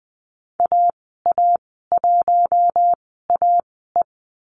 Morse code, heard as a keyed tone, about 20 wpm, spaced as for 14 wpm, 700 Hz. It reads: AA1AE
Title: 042 Lesson 8g - Number 1 - Call Signs 20wpm